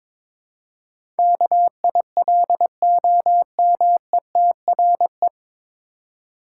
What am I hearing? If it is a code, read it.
Morse code, heard as a keyed tone, about 22 wpm, 700 Hz. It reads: KILOMETRE
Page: Morse Code Ninja practice 22wpm